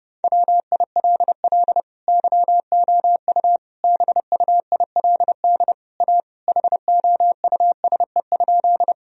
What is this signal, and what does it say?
Morse code, heard as a keyed tone, about 30 wpm, 700 Hz. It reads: WILL YOU BUILD A HOUSE?
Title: Sentences from Top 1000 Words - Review Part 1 - 30wpm